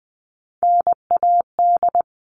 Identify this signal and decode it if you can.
Morse code, heard as a keyed tone, about 20 wpm, 700 Hz. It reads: NAD
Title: All Cut Numbers - 3-Digit 20wpm